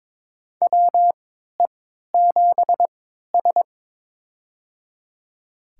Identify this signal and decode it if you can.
Morse code, heard as a keyed tone, about 22 wpm, spaced as for 13 wpm, 700 Hz. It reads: WE7S
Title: USA Call Signs - 4 Characters - 3x Character Spacing - 22wpm